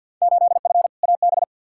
Morse code, heard as a keyed tone, about 51 wpm, 700 Hz. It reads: 8F AB
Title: Field Day Exchanges Encore - 50wpm